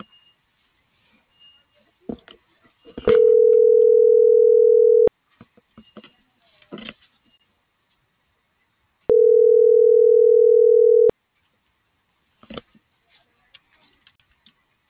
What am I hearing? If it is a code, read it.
Ringback tone (North America), 440 + 480 Hz.